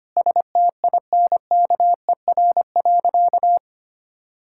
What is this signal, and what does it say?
Morse code, heard as a keyed tone, about 25 wpm, 700 Hz. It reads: STINKER.
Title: Pangram Challenge v2 - 25wpm